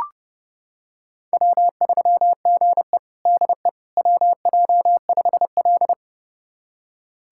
Morse code, heard as a keyed tone, about 30 wpm, 700 Hz. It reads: W3GE DE WJ5L
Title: USA Callsign DE Callsign - 4 Characters - 30wpm